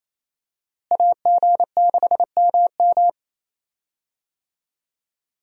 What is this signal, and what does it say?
Morse code, heard as a keyed tone, about 28 wpm, 700 Hz. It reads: AG6MM
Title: USA Call Signs - 5 Characters - Repeated 3x - 28wpm